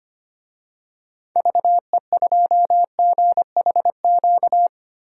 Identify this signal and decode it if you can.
Morse code, heard as a keyed tone, about 25 wpm, 700 Hz. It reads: VE2GHQ
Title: DXCC-100 Call Signs - 6 Characters - 25wpm